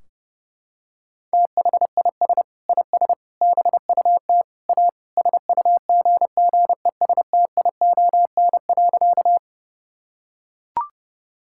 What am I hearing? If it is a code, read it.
Morse code, heard as a keyed tone, about 30 wpm, 700 Hz. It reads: THIS IS BUT A SUGGESTION.
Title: Sentences from Top 1600 Words 30wpm